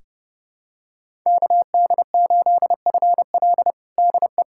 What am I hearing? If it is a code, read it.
Morse code, heard as a keyed tone, about 30 wpm, 700 Hz. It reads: KD8FL DE
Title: USA Callsign DE Callsign - 5 Characters - 30wpm